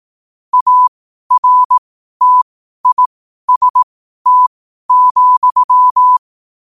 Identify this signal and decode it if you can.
Morse code, heard as a keyed tone, about 18 wpm, spaced as for 12 wpm, 1000 Hz. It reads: ARTIST,
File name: qotdfiles / 12wpm